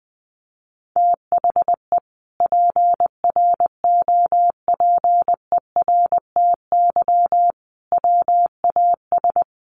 Morse code, heard as a keyed tone, about 20 wpm, 700 Hz. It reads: THE PROPERTY WAS
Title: Sentences from Top 1200 Words 20wpm